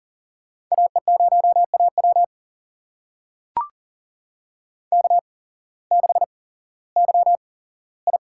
Morse code, heard as a keyed tone, about 40 wpm, spaced as for 19 wpm, 700 Hz. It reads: AE0AW K6YI
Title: USA Call Signs - 8x Character Spacing - 40wpm